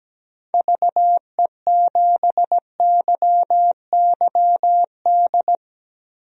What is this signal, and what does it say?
Morse code, heard as a keyed tone, about 17 wpm, 700 Hz. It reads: VE7YYD
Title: DXCC-100 Call Signs - 6 Characters - Repeated 2x - 17wpm